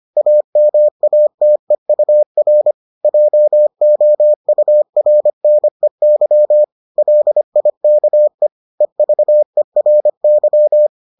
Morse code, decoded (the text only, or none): AMATEUR JOURNEY LIKE EVERY